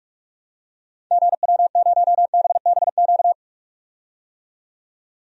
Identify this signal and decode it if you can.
Morse code, heard as a keyed tone, about 43 wpm, 700 Hz. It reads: GW0BBQ
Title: DXCC-100 Call Signs - 6 Characters - 45wpm